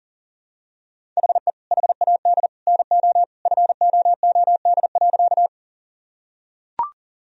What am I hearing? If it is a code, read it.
Morse code, heard as a keyed tone, about 40 wpm, 700 Hz. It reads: HE HAD NO FOOD.
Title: Sentences from Top 1000 Words - Review Part 1 - 40wpm